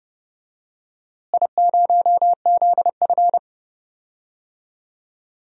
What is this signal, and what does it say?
Morse code, heard as a keyed tone, about 32 wpm, 700 Hz. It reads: I0ZF
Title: DXCC-100 Call Signs - 4 Characters - Repeated 2x - 30wpm